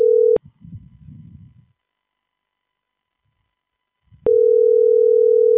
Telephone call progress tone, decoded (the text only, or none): ringback tone